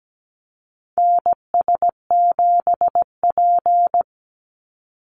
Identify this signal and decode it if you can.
Morse code, heard as a keyed tone, about 17 wpm, 700 Hz. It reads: NS7P
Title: NA SOTA Activator Callsigns 17wpm